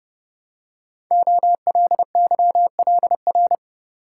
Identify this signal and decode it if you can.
Morse code, heard as a keyed tone, about 28 wpm, 700 Hz. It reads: OLYLR
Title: Code Groups - 5 Letter:Numbers - 30wpm